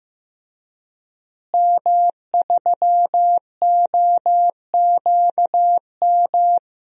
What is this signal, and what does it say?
Morse code, heard as a keyed tone, about 15 wpm, 700 Hz. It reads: M3OQM
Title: DXCC-100 Call Signs - 5 Characters - Repeated 3x - 15wpm